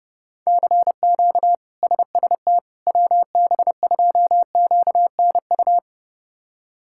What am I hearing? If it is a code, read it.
Morse code, heard as a keyed tone, about 32 wpm, 700 Hz. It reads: CQ SST WB2QNU
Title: K1USN SST QSO Practice - 30wpm